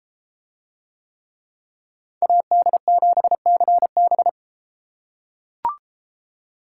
Morse code, heard as a keyed tone, about 32 wpm, 700 Hz. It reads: AD7CB